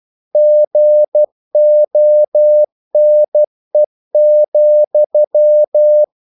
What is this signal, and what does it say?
Morse code, heard as a keyed tone, about 12 wpm, 600 Hz. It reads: GONE,
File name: CW / Practice / If12